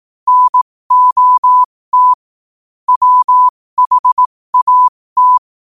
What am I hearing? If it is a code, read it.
Morse code, heard as a keyed tone, about 18 wpm, spaced as for 15 wpm, 1000 Hz. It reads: NOT WHAT